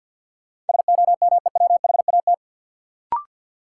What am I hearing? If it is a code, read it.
Morse code, heard as a keyed tone, about 53 wpm, 700 Hz. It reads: SOMEWHAT
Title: Sentences from Top 1000 Words - Review Part 1 - New Word Drill - 50wpm